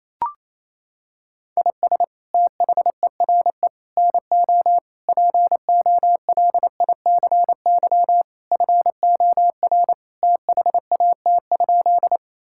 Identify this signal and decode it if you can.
Morse code, heard as a keyed tone, about 28 wpm, 700 Hz. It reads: IS THERE NO POLICY FOR THAT?
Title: Sentences from Top 1000 Words - Review Part 5 - 28wpm